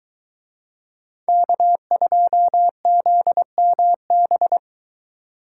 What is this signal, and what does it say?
Morse code, heard as a keyed tone, about 21 wpm, 700 Hz. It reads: K2ZMB